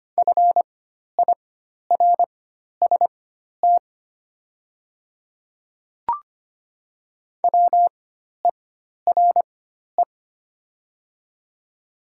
Morse code, heard as a keyed tone, about 25 wpm, spaced as for 12 wpm, 700 Hz. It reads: FIRST WERE
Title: 4x Character Spacing - Top 500 Words - 25wpm